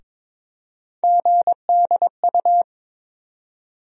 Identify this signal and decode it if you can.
Morse code, heard as a keyed tone, about 22 wpm, 700 Hz. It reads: GDU
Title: USA Call Sign Postfixes - 3 Letters 22wpm